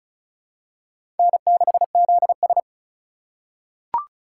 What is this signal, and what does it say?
Morse code, heard as a keyed tone, about 34 wpm, 700 Hz. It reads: N6ZS
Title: USA Call Signs - 4 Characters - 2x Character Spacing - 35wpm